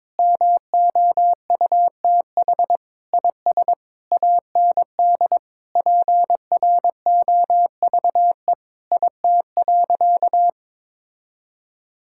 Morse code, heard as a keyed tone, about 22 wpm, 700 Hz. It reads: MOUTH IS AND PROVE IT.
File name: practice / 22wpm / Sentences with 8-Word Phrases - 22wpm